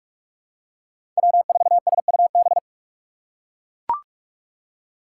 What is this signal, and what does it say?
Morse code, heard as a keyed tone, about 43 wpm, 690 Hz. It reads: W4SUB
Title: USA Call Signs - 4x Character Spacing - 45wpm